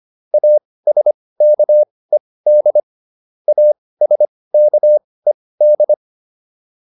Morse code, heard as a keyed tone, about 25 wpm, spaced as for 18 wpm, 600 Hz. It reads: ASKED ASKED